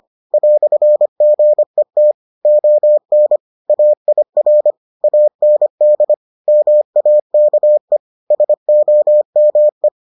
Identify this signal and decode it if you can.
Morse code, heard as a keyed tone, about 25 wpm, 600 Hz. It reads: "GET ON AIR AND MAKE SOME